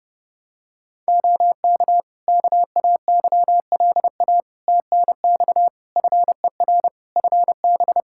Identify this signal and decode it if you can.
Morse code, heard as a keyed tone, about 30 wpm, 700 Hz. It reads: OK KAYLA TNX FER FB
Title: K1USN SST QSO Practice 30wpm